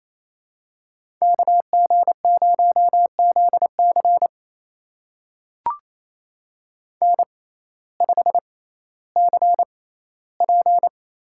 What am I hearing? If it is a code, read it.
Morse code, heard as a keyed tone, about 28 wpm, spaced as for 15 wpm, 700 Hz. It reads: KG0ZC N5CP
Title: USA Call Signs - 5 Characters - 6x Character Spacing - 28wpm